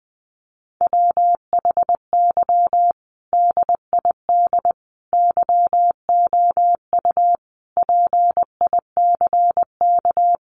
Morse code, heard as a keyed tone, about 20 wpm, 700 Hz. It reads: WHY DID YOU PICK
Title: Sentences from Top 1000 Words 20wpm